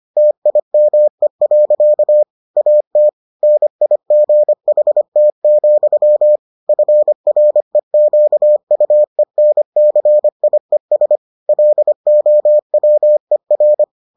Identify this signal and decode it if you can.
Morse code, heard as a keyed tone, about 25 wpm, 600 Hz. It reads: TIME. AT NIGHT, FREQUENCIES LOWER